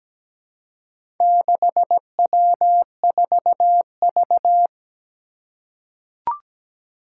Morse code, heard as a keyed tone, about 17 wpm, 700 Hz. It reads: 6W4V
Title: Four Challenging Letter-Number Combinations 17wpm